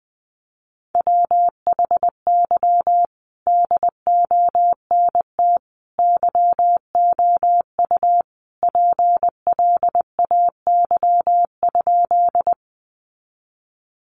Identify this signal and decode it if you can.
Morse code, heard as a keyed tone, about 20 wpm, 700 Hz. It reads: WHY DONT YOU PLAY?
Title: Sentences from Top 300 Words 20wpm